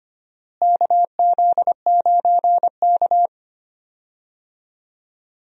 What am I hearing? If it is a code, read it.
Morse code, heard as a keyed tone, about 25 wpm, 700 Hz. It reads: KZ9K
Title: USA Call Signs - 4 Characters - Repeated 3x - 25wpm